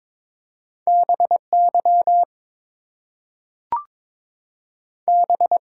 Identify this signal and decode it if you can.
Morse code, heard as a keyed tone, about 22 wpm, 700 Hz. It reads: BY B
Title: Two Letter Words - 22wpm